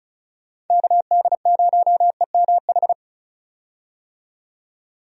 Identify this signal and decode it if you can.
Morse code, heard as a keyed tone, about 36 wpm, 700 Hz. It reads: KD0EMH
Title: USA Call Signs - 6 Characters - 35wpm